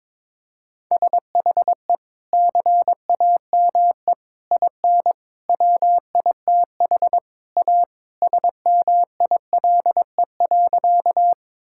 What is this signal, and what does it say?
Morse code, heard as a keyed tone, about 22 wpm, 700 Hz. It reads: SHE CAME IN WITH A SMILE.
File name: practice / 22wpm / Sentences with 2-Word Phrases - 22wpm